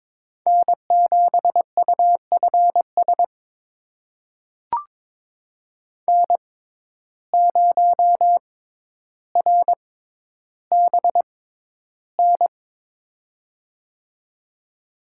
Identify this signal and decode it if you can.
Morse code, heard as a keyed tone, about 22 wpm, spaced as for 11 wpm, 700 Hz. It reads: N7UFS N0RBN
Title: USA Call Signs - 5 Characters - 6x Character Spacing - 22wpm